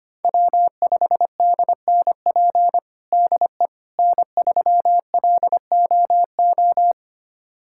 Morse code, heard as a keyed tone, about 25 wpm, 700 Hz. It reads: W5DNP DE N3LOO
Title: USA Callsign DE Callsign - 5 Characters - 25wpm